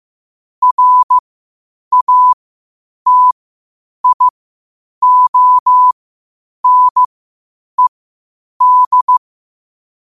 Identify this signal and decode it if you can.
Morse code, heard as a keyed tone, about 15 wpm, spaced as for 8 wpm, 1000 Hz. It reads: RATIONED
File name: qotdfiles / 8wpm